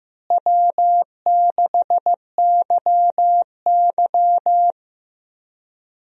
Morse code, heard as a keyed tone, about 15 wpm, 700 Hz. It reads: W6YY
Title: USA Call Signs - 4 Characters - Repeated 2x - 15wpm